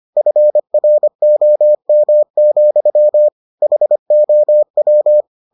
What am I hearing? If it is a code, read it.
Morse code, heard as a keyed tone, about 25 wpm, 600 Hz. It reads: FROM, HOW